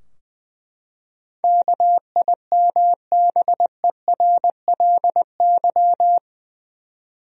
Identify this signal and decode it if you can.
Morse code, heard as a keyed tone, about 20 wpm, 700 Hz. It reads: KIMBERLY
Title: USA Common Names - 20wpm